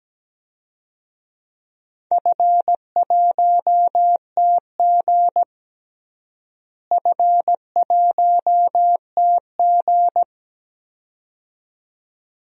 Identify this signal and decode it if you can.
Morse code, heard as a keyed tone, about 17 wpm, 700 Hz. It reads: F1TG F1TG
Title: DXCC-100 Call Signs - 4 Characters - Repeated 2x - 17wpm